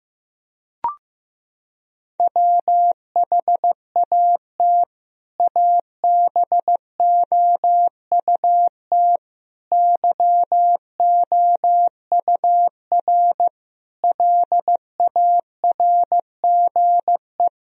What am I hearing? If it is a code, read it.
Morse code, heard as a keyed tone, about 16 wpm, 700 Hz. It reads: WHAT ABOUT YOUR LARGE
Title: Sentences from Top 1000 Words - Review Part 1 - 15wpm